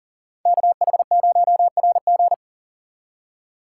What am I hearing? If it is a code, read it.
Morse code, heard as a keyed tone, about 48 wpm, 700 Hz. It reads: KH0RG
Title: Speed Racing - USA 2x2 Call Signs 40wpm